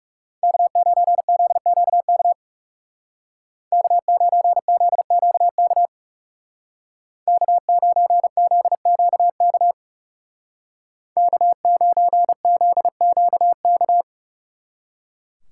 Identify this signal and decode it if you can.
Morse code, heard as a keyed tone, about 37 wpm, 700 Hz. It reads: K9ZQK K9ZQK K9ZQK K9ZQK
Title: Speed Racing - USA 1x3 Call Signs 30wpm